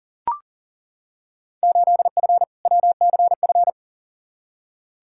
Morse code, heard as a keyed tone, about 40 wpm, 700 Hz. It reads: 8F WCF
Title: Field Day Exchanges Encore - 40wpm